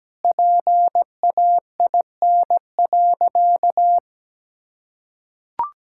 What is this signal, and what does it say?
Morse code, heard as a keyed tone, about 17 wpm, 700 Hz. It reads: PAIN.
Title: Sentences from Top 1600 Words 17wpm